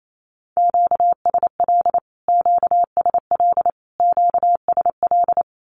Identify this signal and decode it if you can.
Morse code, heard as a keyed tone, about 28 wpm, 700 Hz. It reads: QSL QSL QSL
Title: Q-Codes Repeated 3x 28wpm